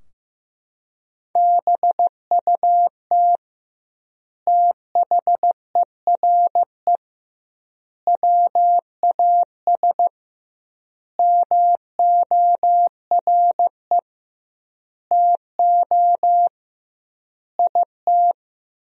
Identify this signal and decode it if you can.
Morse code, heard as a keyed tone, about 16 wpm, 700 Hz. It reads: BUT THERE WAS MORE TO IT
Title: 2x Word-Spacing - Sentences from Top 100 Words - 15wpm